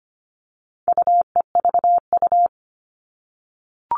Morse code, heard as a keyed tone, about 25 wpm, 700 Hz. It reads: UEVU
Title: All Cut Numbers - 4-Digit 25wpm